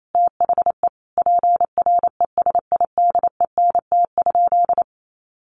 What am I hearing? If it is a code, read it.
Morse code, heard as a keyed tone, about 28 wpm, 700 Hz. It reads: THE PRESIDENT?
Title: Sentences from Top 1300 Words 28wpm